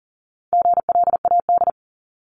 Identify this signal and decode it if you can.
Morse code, heard as a keyed tone, about 39 wpm, 700 Hz. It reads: GLAD
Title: Multi-letter Phonemes 40wpm